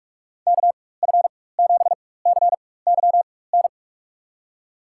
Morse code, heard as a keyed tone, about 42 wpm, spaced as for 20 wpm, 700 Hz. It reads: KF7CYN
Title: USA Call Signs - 6 Characters - 4x Character Spacing - 45wpm